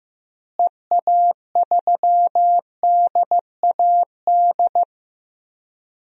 Morse code, heard as a keyed tone, about 16 wpm, 700 Hz. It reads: EA3DAD
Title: DXCC-100 Call Signs - 6 Characters - Repeated 3x - 15wpm